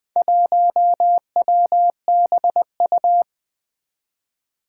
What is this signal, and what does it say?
Morse code, heard as a keyed tone, about 20 wpm, 700 Hz. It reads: 1WBU
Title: Four Challenging Letter-Number Combinations 20wpm